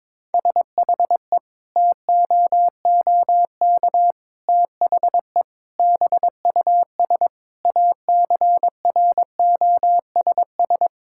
Morse code, heard as a keyed tone, about 22 wpm, 700 Hz. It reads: SHE TOOK THE BUS ACROSS
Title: Sentences from Top 1000 Words - Review Part 6 - 22wpm